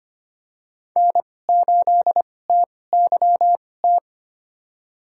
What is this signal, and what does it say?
Morse code, heard as a keyed tone, about 25 wpm, spaced as for 18 wpm, 700 Hz. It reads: N8TYT
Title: USA Call Signs - 5 Characters - 2x Character Spacing - 25wpm